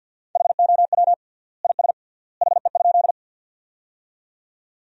Morse code, heard as a keyed tone, about 50 wpm, 700 Hz. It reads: HOW IS HE?
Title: 3x Word-Spacing - Sentences from Top 100 Words - 50wpm